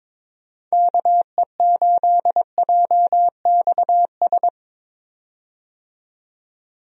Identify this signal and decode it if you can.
Morse code, heard as a keyed tone, about 22 wpm, 700 Hz. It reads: KE8JXS
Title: USA Call Signs - 6 Characters - 22wpm